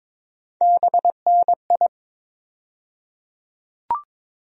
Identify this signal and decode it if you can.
Morse code, heard as a keyed tone, about 21 wpm, 700 Hz. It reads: BNI
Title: USA Call Sign Postfixes - 3 Letters 22wpm